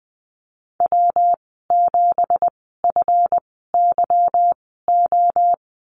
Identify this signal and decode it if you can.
Morse code, heard as a keyed tone, about 20 wpm, spaced as for 15 wpm, 700 Hz. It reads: W7FYO
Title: 193 Lesson 31g - Letter v - Call Signs 20wpm